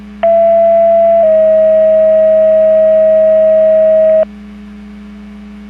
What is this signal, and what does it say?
Two-tone page, heard as a one-second tone, then a three-second tone, about 670 then 640 Hz. It is heard from 0.2 to 4.2 s.
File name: 10-47 Portage 154.385 Paging